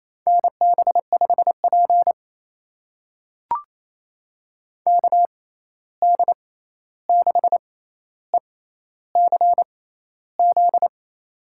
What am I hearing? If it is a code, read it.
Morse code, heard as a keyed tone, about 28 wpm, spaced as for 13 wpm, 700 Hz. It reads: NB5P KD6ECZ
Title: USA Call Signs - 6x Character Spacing - 28wpm